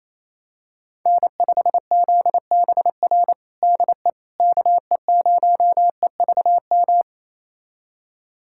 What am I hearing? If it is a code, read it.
Morse code, heard as a keyed tone, about 28 wpm, 700 Hz. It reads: N5ZBR DE KE0EVM
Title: USA Callsign DE Callsign 28wpm